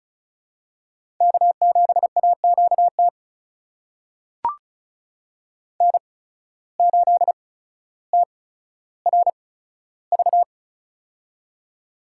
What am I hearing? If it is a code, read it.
Morse code, heard as a keyed tone, about 36 wpm, spaced as for 15 wpm, 700 Hz. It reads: K7AQT N8TRV
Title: USA Call Signs - 5 Characters - 8x Character Spacing - 35wpm